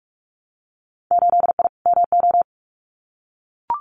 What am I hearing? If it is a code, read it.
Morse code, heard as a keyed tone, about 44 wpm, 700 Hz. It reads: 8I MO
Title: Winter Field Day Exchanges Encore 45wpm